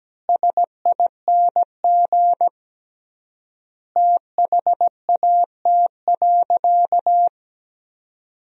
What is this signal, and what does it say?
Morse code, heard as a keyed tone, about 17 wpm, 700 Hz. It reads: SING THAT.
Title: 3x Word-Spacing - Sentences from Top 500 Words - 17wpm